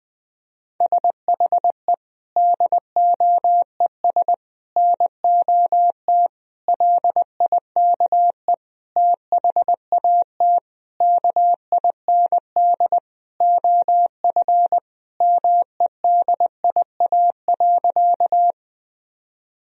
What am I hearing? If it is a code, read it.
Morse code, heard as a keyed tone, about 20 wpm, 700 Hz. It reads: SHE DOES NOT LIKE THAT KIND OF MEDIA.
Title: Sentences from Top 1000 Words - Review Part 4 - 20wpm